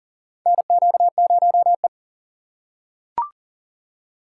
Morse code, heard as a keyed tone, about 40 wpm, 700 Hz. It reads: NQ0E
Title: USA Call Signs - 4 Characters - 2x Character Spacing - 40wpm